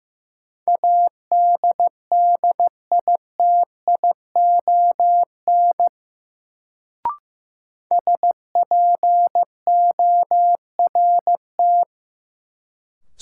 Morse code, heard as a keyed tone, about 15 wpm, 700 Hz. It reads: ADDITION SPORT
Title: Sentences from Top 1500 Words - New Word Drill Alt - 15wpm